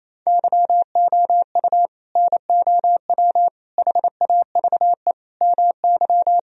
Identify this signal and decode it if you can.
Morse code, heard as a keyed tone, about 28 wpm, 700 Hz. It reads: YOU NOW HAVE MY
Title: Sentences from Top 1000 Words - Review Part 1 - 28wpm